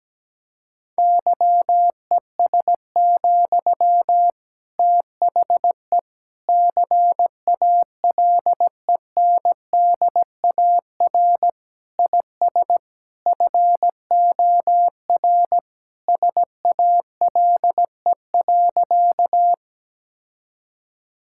Morse code, heard as a keyed tone, about 17 wpm, 700 Hz. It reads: YES, THE CALENDAR IS FOR SALE.
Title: Sentences from Top 1600 Words 17wpm